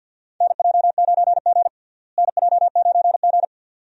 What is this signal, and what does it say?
Morse code, heard as a keyed tone, about 52 wpm, 700 Hz. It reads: NJ9G NJ9G
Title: USA Call Signs - 4 Characters - Repeated 2x - 50wpm